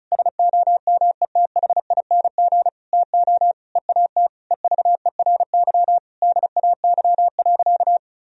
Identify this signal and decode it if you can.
Morse code, heard as a keyed tone, about 34 wpm, 700 Hz. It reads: SOMETHING TO EAT EVERY DAY.
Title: Sentences from Top 1000 Words - Review Part 4 - 35wpm